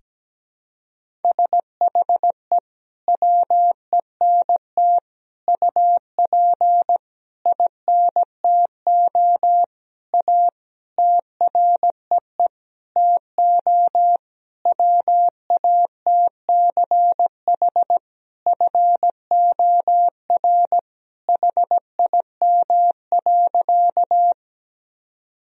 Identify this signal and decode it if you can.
Morse code, heard as a keyed tone, about 17 wpm, 700 Hz. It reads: SHE WENT UP INTO A TREE TO WATCH FOR HIM.
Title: Sentences from Top 1000 Words - Review Part 1 - 17wpm